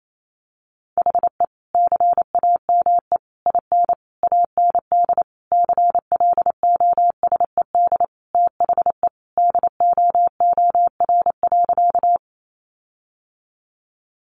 Morse code, heard as a keyed tone, about 28 wpm, 700 Hz. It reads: HE CAME IN AND CLOSED THE DOOR.
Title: Sentences from Top 1400 Words 28wpm